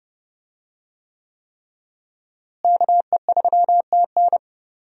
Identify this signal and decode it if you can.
Morse code, heard as a keyed tone, about 30 wpm, 700 Hz. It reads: KE3TN
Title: USA Call Signs - 5 Characters - Repeated 3x - 30wpm